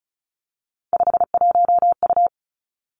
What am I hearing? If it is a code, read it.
Morse code, heard as a keyed tone, about 36 wpm, 700 Hz. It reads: 51U
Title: Three Challenging Letter-Number Combinations 35wpm